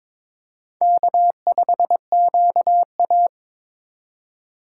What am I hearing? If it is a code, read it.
Morse code, heard as a keyed tone, about 22 wpm, 700 Hz. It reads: K5QA
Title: USA Call Signs - 4 Characters - 2x Character Spacing - 22wpm